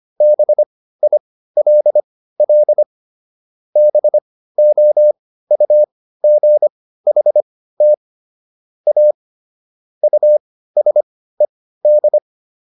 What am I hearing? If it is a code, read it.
Morse code, heard as a keyed tone, about 25 wpm, spaced as for 15 wpm, 600 Hz. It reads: BILL BOUGHT A USED